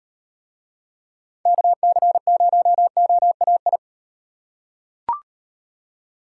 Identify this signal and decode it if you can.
Morse code, heard as a keyed tone, about 38 wpm, 700 Hz. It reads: KC0OAI